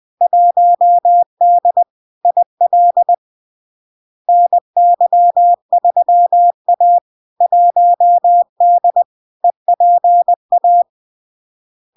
Morse code, heard as a keyed tone, about 20 wpm, 700 Hz. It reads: 1D IL NY3A 1D EPA
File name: LARK-FD-2022-Log-in-CW-20wpm-700-Hz